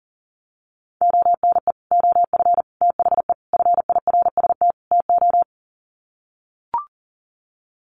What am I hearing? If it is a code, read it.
Morse code, heard as a keyed tone, about 40 wpm, 700 Hz. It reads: ONE OF THE FIRST TO
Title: Sets of 5 Words Encore 40wpm